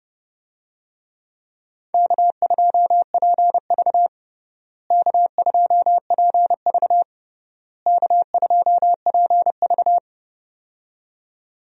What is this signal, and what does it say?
Morse code, heard as a keyed tone, about 32 wpm, 700 Hz. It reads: K2PV K2PV K2PV
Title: USA Call Signs - 4 Characters - Repeated 3x - 30wpm